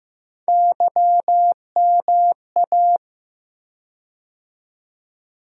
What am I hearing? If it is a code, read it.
Morse code, heard as a keyed tone, about 15 wpm, 700 Hz. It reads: YMA
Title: USA Call Sign Postfixes - 3 Letters 15wpm